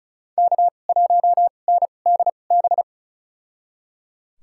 Morse code, heard as a keyed tone, about 36 wpm, spaced as for 26 wpm, 700 Hz. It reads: K1NDB
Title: USA Call Signs - 5 Characters - 2x Character Spacing - 35wpm